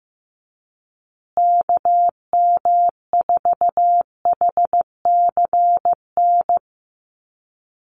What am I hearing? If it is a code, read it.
Morse code, heard as a keyed tone, about 15 wpm, 700 Hz. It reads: KM4HCN
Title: USA Call Signs Repeated 2x 15wpm